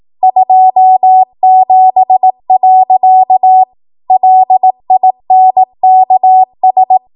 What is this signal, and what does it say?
Morse code, heard as a keyed tone, about 18 wpm, 750 Hz. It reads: 27. LINKS
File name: Morse / 18WPM